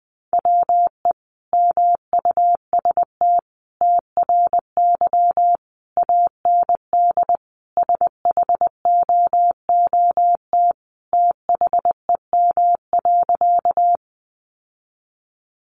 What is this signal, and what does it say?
Morse code, heard as a keyed tone, about 20 wpm, 700 Hz. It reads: WE MUST TRY AND SHOOT THEM.
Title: Sentences from Top 1200 Words 20wpm